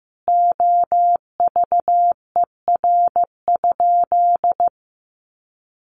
Morse code, heard as a keyed tone, about 14 wpm, 700 Hz. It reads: OVER?
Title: Sentences from Top 1500 Words 15wpm